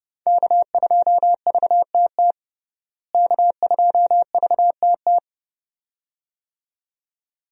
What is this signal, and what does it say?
Morse code, heard as a keyed tone, about 32 wpm, 700 Hz. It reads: K2VTT K2VTT
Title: USA Call Signs - 5 Characters - Repeated 2x - 30wpm